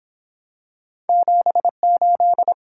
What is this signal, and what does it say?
Morse code, heard as a keyed tone, about 28 wpm, 700 Hz. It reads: 78